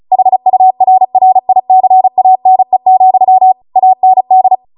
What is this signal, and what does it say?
Morse code, heard as a keyed tone, about 36 wpm, 750 Hz. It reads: HURRICANE, AND